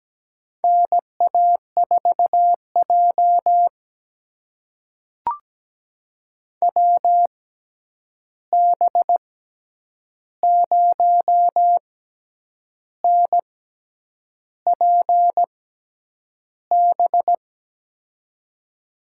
Morse code, heard as a keyed tone, about 17 wpm, spaced as for 8 wpm, 700 Hz. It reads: NA4J WB0NPB
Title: USA Call Signs - 6x Character Spacing - 17wpm